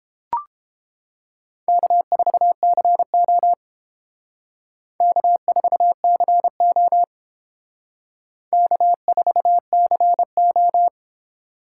Morse code, heard as a keyed tone, about 28 wpm, 700 Hz. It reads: K4CO K4CO K4CO